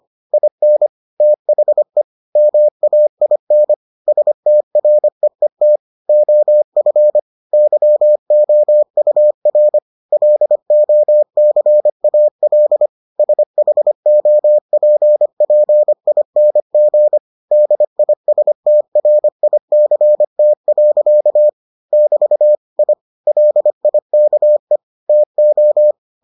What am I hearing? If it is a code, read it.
Morse code, heard as a keyed tone, about 25 wpm, 600 Hz. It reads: IN THE MAIN STREET OF YOUR LOCAL SHOPPING DISTRICT. <BT> I LIKE TO